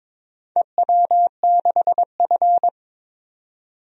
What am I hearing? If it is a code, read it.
Morse code, heard as a keyed tone, about 21 wpm, 700 Hz. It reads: EW6F
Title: DXCC-100 Call Signs - 4 Characters - 22wpm